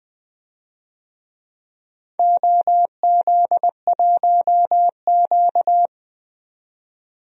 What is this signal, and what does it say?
Morse code, heard as a keyed tone, about 20 wpm, 700 Hz. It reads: OZ1Q
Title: DXCC-100 Call Signs - 4 Characters - 20wpm